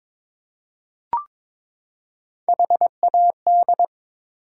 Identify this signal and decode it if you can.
Morse code, heard as a keyed tone, about 22 wpm, 700 Hz. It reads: HAD
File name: Build A Phrase - 2-Word Phrases - 22wpm